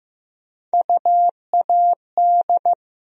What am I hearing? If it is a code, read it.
Morse code, heard as a keyed tone, about 15 wpm, 700 Hz. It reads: UAD
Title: USA Call Sign Postfixes - 3 Letters 15wpm